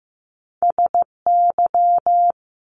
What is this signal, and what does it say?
Morse code, heard as a keyed tone, about 15 wpm, 700 Hz. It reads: SY
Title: Top Two Letter Combinations in Words 15wpm